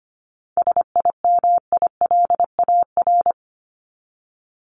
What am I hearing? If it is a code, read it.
Morse code, heard as a keyed tone, about 25 wpm, 700 Hz. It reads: SIMILAR
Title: LY Suffix - Top 100 Words with Base Words 25wpm